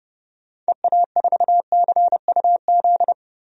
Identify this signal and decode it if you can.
Morse code, heard as a keyed tone, about 30 wpm, 700 Hz. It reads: EA4CUZ
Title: DXCC-100 Call Signs - 6 Characters - 30wpm